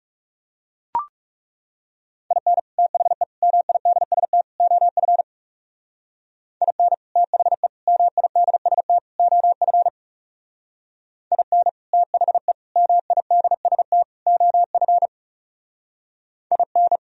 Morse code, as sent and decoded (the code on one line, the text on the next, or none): .. -. / - .... / -- .. -.. ... - / --- ..-. / .. -. / - .... / -- .. -.. ... - / --- ..-. / .. -. / - .... / -- .. -.. ... - / --- ..-. / .. -.
IN THE MIDST OF IN THE MIDST OF IN THE MIDST OF IN